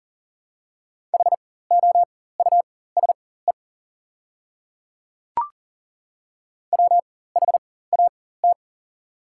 Morse code, heard as a keyed tone, about 40 wpm, spaced as for 19 wpm, 700 Hz. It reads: HOUSE WHAT
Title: 4x Character Spacing - Top 200 Words - 40wpm